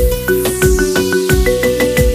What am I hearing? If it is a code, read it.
There is no code.